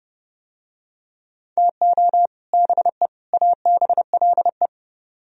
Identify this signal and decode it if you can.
Morse code, heard as a keyed tone, about 31 wpm, 700 Hz. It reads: TO BE ABLE